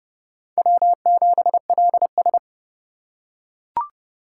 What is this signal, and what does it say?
Morse code, heard as a keyed tone, about 32 wpm, 700 Hz. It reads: W7LS
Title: USA Call Signs - 4 Characters - 4x Character Spacing - 30wpm